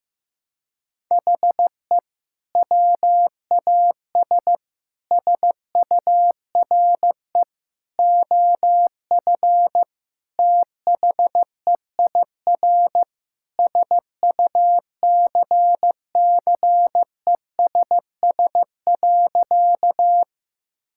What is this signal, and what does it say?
Morse code, heard as a keyed tone, about 15 wpm, 700 Hz. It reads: HE WAS SURE OF THEIR SUCCESS.
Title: Sentences from Top 1000 Words - Review Part 6 - 15wpm